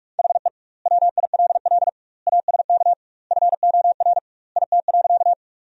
Morse code, heard as a keyed tone, about 46 wpm, 700 Hz. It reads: HE WILL ASK FOR IT.
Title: 2x Word-Spacing - Sentences from Top 200 Words - 45wpm